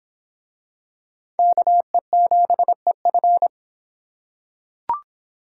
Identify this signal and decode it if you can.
Morse code, heard as a keyed tone, about 24 wpm, 700 Hz. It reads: KE7EF